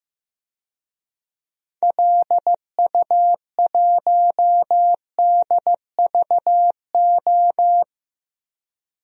Morse code, heard as a keyed tone, about 15 wpm, 700 Hz. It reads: LU1DVO
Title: DXCC-100 Call Signs - 6 Characters - 15wpm